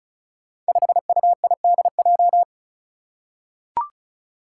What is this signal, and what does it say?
Morse code, heard as a keyed tone, about 36 wpm, 700 Hz. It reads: 5UIDJ
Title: Five Challenging Letter-Number Combinations 35wpm